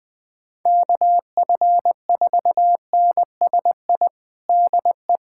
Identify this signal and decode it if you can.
Morse code, heard as a keyed tone, about 20 wpm, 700 Hz. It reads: KF4NSI DE
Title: USA Callsign DE Callsign - 6 Characters - 20wpm